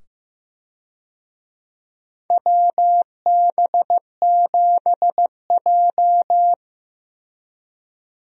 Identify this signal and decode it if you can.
Morse code, heard as a keyed tone, about 15 wpm, 700 Hz. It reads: WB7J
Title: USA Call Signs - 4 Characters - 15wpm